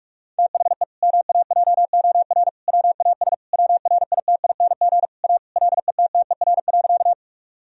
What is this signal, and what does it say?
Morse code, heard as a keyed tone, about 41 wpm, 700 Hz. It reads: THE MAJOR WAS WRITING A LETTER.
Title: Sentences from Top 1000 Words - Review Part 4 - 45wpm